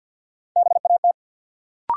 Morse code, heard as a keyed tone, about 52 wpm, 700 Hz. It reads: BAT
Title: Sentences from Top 1500 Words - New Word Drill - 50wpm